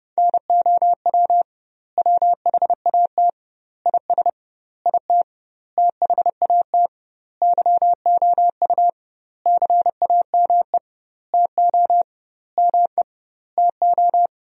Morse code, heard as a keyed tone, about 32 wpm, 700 Hz. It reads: NOW WHAT IS IT THAT YOU CAME TO ME TO
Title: 2x Word-Spacing - Sentences from Top 200 Words - 30wpm